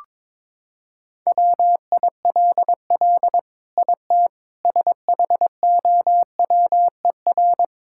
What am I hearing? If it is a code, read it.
Morse code, heard as a keyed tone, about 22 wpm, 700 Hz. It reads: WILL IT SHOWER
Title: Sentences from Top 1600 Words 22wpm